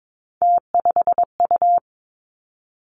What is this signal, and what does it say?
Morse code, heard as a keyed tone, about 22 wpm, 700 Hz. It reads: T5U
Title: Three Challenging Letter-Number Combinations 22wpm